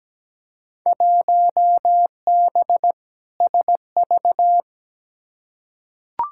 Morse code, heard as a keyed tone, about 17 wpm, 700 Hz. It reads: 1B SV
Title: Field Day Exchanges - 17wpm